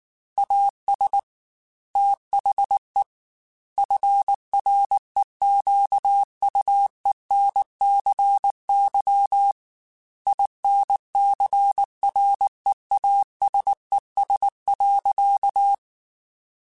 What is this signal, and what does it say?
Morse code, heard as a keyed tone, about 19 wpm, 800 Hz. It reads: AS THE FREQUENCY INCREASES.
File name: Technical Subjects / Technical Subjects-mp3s / XC @18